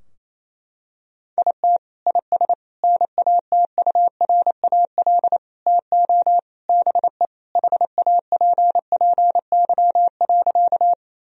Morse code, heard as a keyed tone, about 28 wpm, 700 Hz. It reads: IT IS NATURAL TO BE HAPPY.
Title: Sentences from Top 1000 Words - Review Part 4 - 28wpm